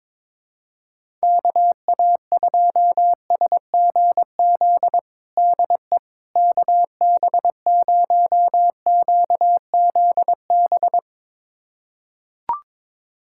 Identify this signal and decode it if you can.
Morse code, heard as a keyed tone, about 22 wpm, 700 Hz. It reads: KA2SGZ DE KB0QZB
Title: USA Callsign DE Callsign 22wpm